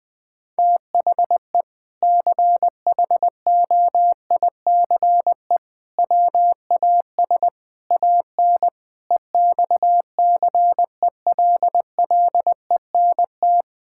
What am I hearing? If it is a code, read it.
Morse code, heard as a keyed tone, about 20 wpm, 700 Hz. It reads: THE CHOICE WAS AN EXCELLENT
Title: Sentences from Top 1000 Words - Review Part 6 - 20wpm